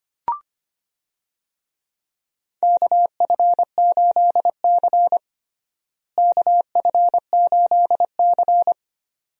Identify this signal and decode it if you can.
Morse code, heard as a keyed tone, about 25 wpm, 700 Hz. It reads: KF8C KF8C
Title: USA Call Signs - 4 Characters - Repeated 2x - 25wpm